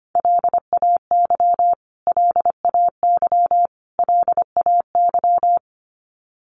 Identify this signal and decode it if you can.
Morse code, heard as a keyed tone, about 25 wpm, 700 Hz. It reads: LAY LAY LAY
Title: Top 500 Words Repeated 25wpm